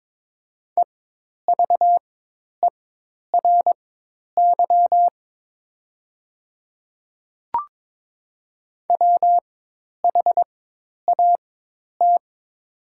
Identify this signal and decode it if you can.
Morse code, heard as a keyed tone, about 23 wpm, spaced as for 10 wpm, 700 Hz. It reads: EVERY WHAT
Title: 4x Character Spacing - Top 200 Words - 22wpm